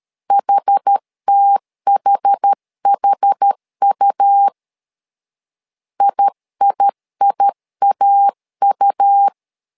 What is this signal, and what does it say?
Morse code, heard as a keyed tone, about 13 wpm, 800 Hz. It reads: HTHHU IIIAU
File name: LEZIONI / 2 UHDN / 2 UH 12wpm 1439